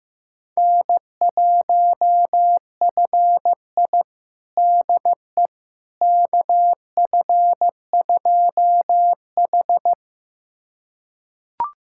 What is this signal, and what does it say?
Morse code, heard as a keyed tone, about 15 wpm, 700 Hz. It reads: N1FI DE KF2H
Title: USA Callsign DE Callsign - 4 Characters - 15wpm